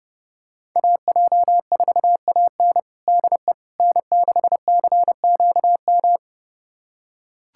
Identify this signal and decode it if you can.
Morse code, heard as a keyed tone, about 30 wpm, 700 Hz. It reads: AJ4AN DE N6CQM
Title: USA Callsign DE Callsign - 5 Characters - 30wpm